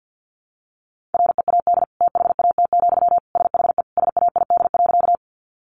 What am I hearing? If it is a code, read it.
Morse code, heard as a keyed tone, about 50 wpm, 700 Hz. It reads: READ THAT, SHE SAID.
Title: Sentences from Top 200 Words 50wpm